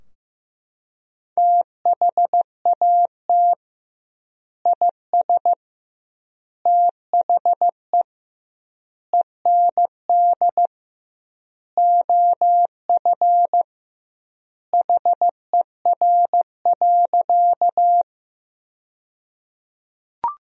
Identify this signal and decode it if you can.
Morse code, heard as a keyed tone, about 15 wpm, 700 Hz. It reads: THAT IS THE END OF HER.
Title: 2x Word-Spacing - Sentences from Top 200 Words - 15wpm